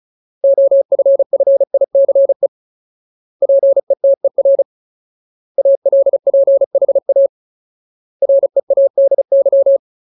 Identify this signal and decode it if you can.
Morse code, heard as a keyed tone, about 35 wpm, 550 Hz. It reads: OFFICE PETER ALPHA READY
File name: Ultimate Word List / 35WPM Ultimate